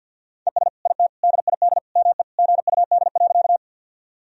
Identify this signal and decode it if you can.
Morse code, heard as a keyed tone, about 51 wpm, 700 Hz. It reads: ES IT DID ME GUD.
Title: Sentences from Top 200 Words and Abbreviations 50wpm